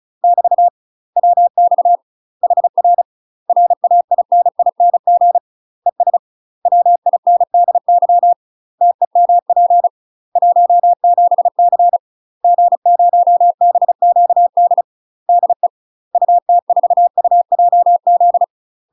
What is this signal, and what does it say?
Morse code, heard as a keyed tone, about 36 wpm, 700 Hz. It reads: <BT> WX HR RAINING ES WINDY TEMP 17C G0BQD DE UT4UJZ
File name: UK MASTER QSOs TRAINING / 35WPM / UK MASTER QSOs 0022